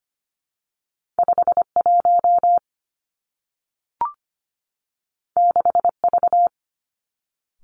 Morse code, heard as a keyed tone, about 25 wpm, 700 Hz. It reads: 5 1 6 V
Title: Two Challenging Letter-Number Combinations 25wpm